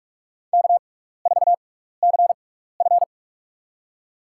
Morse code, heard as a keyed tone, about 46 wpm, spaced as for 15 wpm, 700 Hz. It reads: K4CF
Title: USA Call Signs - 4 Characters - 6x Character Spacing - 45wpm